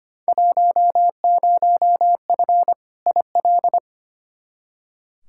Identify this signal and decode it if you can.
Morse code, heard as a keyed tone, about 25 wpm, 700 Hz. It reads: 10F IL
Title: Field Day Exchanges Encore - 25wpm